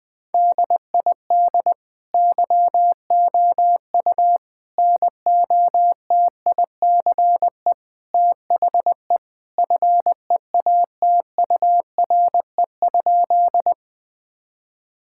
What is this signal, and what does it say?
Morse code, heard as a keyed tone, about 20 wpm, 700 Hz. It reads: DID YOU NOTICE THE FEATURE?
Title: Sentences from Top 1000 Words - Review Part 5 - 20wpm